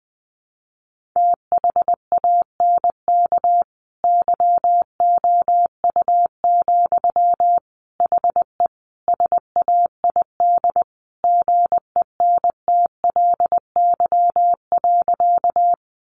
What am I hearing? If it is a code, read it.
Morse code, heard as a keyed tone, about 20 wpm, 700 Hz. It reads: THANK YOU, HE SAID GENTLY.
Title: Sentences from Top 1500 Words 20wpm